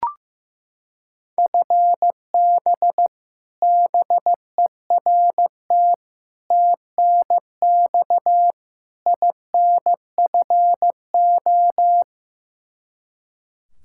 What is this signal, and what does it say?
Morse code, heard as a keyed tone, about 15 wpm, 700 Hz. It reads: FB BERT TNX INFO
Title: Practice QSO - Exchange 3 - More Info - 15wpm